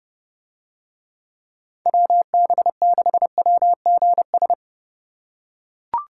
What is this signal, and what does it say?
Morse code, heard as a keyed tone, about 29 wpm, 700 Hz. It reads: WB6WGS